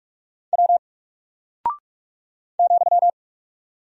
Morse code, heard as a keyed tone, about 43 wpm, 700 Hz. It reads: W ,
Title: Single Character (Letter:Number:Punctuation:Prosign) Alt 45wpm